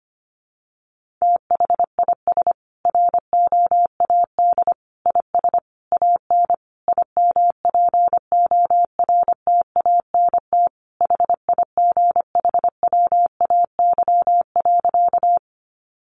Morse code, heard as a keyed tone, about 25 wpm, 700 Hz. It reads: THIS ROAD IS AN IMPORTANT HIGHWAY.
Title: Sentences from Top 1400 Words 25wpm